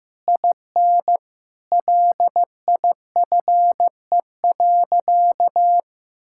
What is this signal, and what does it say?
Morse code, heard as a keyed tone, about 15 wpm, 700 Hz. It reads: IN LIFE.
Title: Sentences from Top 1000 Words - Review Part 6 - 15wpm